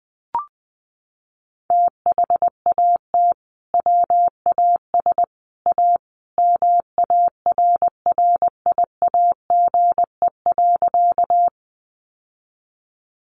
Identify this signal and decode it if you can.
Morse code, heard as a keyed tone, about 20 wpm, 700 Hz. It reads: THAT WAS A MARRIAGE.
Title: Sentences from Top 800 Words 20wpm